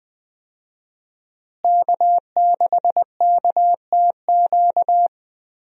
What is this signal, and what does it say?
Morse code, heard as a keyed tone, about 20 wpm, 700 Hz. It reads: K6KTQ
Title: USA Call Signs - 5 Characters - Repeated 3x - 20wpm